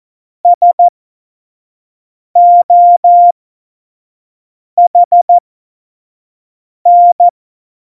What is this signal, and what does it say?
Morse code, heard as a keyed tone, about 14 wpm, spaced as for 5 wpm, 700 Hz. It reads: SOHN
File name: Morsekurs / Witze / Witz 05. Juni 2018, 25 F70